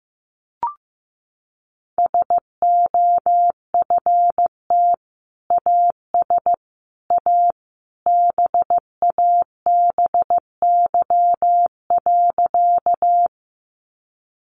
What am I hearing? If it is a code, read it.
Morse code, heard as a keyed tone, about 15 wpm, 700 Hz. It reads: SOFT AS A BABY.
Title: Sentences from Top 1500 Words 15wpm